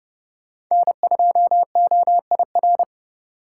Morse code, heard as a keyed tone, about 30 wpm, 700 Hz. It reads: N2OIR
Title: USA Call Signs - 5 Characters - 3x Character Spacing - 30wpm